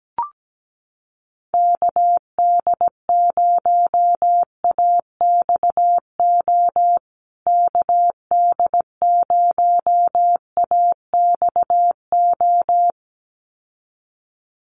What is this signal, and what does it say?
Morse code, heard as a keyed tone, about 17 wpm, 700 Hz. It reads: KD0AXO KD0AXO
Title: USA Call Signs Repeated 2x 17wpm